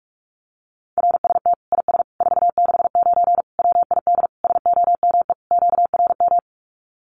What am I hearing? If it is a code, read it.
Morse code, heard as a keyed tone, about 44 wpm, 700 Hz. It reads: RST IS 469 WID SOME QRM
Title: Common QSO Phrases 45wpm